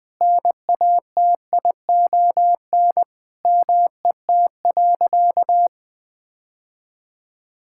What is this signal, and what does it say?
Morse code, heard as a keyed tone, about 21 wpm, 700 Hz. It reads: NATION MET.
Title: Sentences with 3-Word Phrases - 20wpm